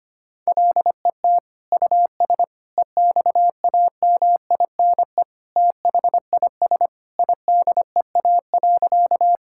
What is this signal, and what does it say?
Morse code, heard as a keyed tone, about 25 wpm, 700 Hz. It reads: LET US EXAMINE THIS IDEA.
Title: Sentences from Top 1600 Words 25wpm